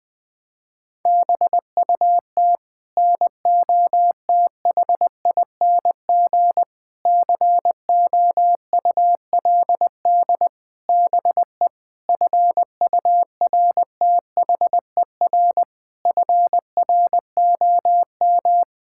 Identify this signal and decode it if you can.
Morse code, heard as a keyed tone, about 20 wpm, 700 Hz. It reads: BUT NOTHING COULD BE FURTHER FROM
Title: Sets of 8 Words - 20wpm